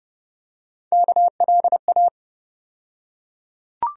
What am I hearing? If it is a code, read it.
Morse code, heard as a keyed tone, about 30 wpm, 700 Hz. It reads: KLA
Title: USA Call Sign Postfixes - 3 Letters 30wpm